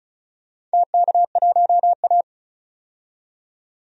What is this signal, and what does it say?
Morse code, heard as a keyed tone, about 36 wpm, 700 Hz. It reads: TK1A
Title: DXCC-100 Call Signs - 4 Characters - 35wpm